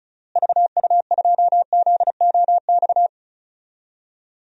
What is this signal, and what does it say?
Morse code, heard as a keyed tone, about 36 wpm, 700 Hz. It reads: VU2ZOX
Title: DXCC-100 Call Signs - 6 Characters - Repeated 2x - 35wpm